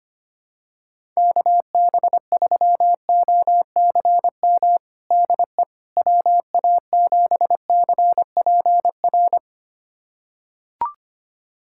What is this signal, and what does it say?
Morse code, heard as a keyed tone, about 25 wpm, 700 Hz. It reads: KB3OCM DE WA7CPR
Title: USA Callsign DE Callsign 25wpm